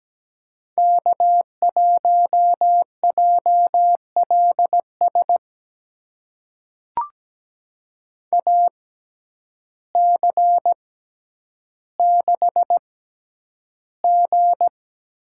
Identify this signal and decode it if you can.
Morse code, heard as a keyed tone, about 17 wpm, spaced as for 9 wpm, 700 Hz. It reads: K1JLS AC6G
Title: USA Call Signs - 5 Characters - 6x Character Spacing - 17wpm